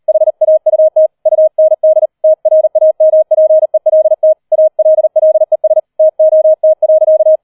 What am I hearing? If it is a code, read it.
Morse code, heard as a keyed tone, about 38 wpm, 630 Hz. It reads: HAUT UND TRAMPELT ALLES TOT.
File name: 40wpm